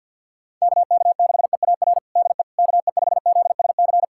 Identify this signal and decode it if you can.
Morse code, heard as a keyed tone, about 51 wpm, 700 Hz. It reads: KK6EAR DE KE5ZSC
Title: USA Callsign DE Callsign 50wpm